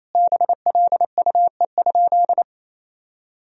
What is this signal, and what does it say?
Morse code, heard as a keyed tone, about 28 wpm, 700 Hz. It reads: BLUE?
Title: Sentences from Top 1000 Words - Review Part 3 - 28wpm